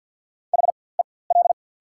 Morse code, heard as a keyed tone, about 54 wpm, spaced as for 24 wpm, 690 Hz. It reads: HEL
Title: 4x Character Spacing - Top 500 Words - 50wpm